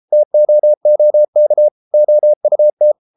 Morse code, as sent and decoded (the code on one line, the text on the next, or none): - --- --- -.- / --- ..- -
TOOK OUT